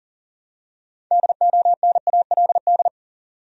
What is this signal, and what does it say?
Morse code, heard as a keyed tone, about 40 wpm, 700 Hz. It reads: DONALD
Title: USA Common Names - 40wpm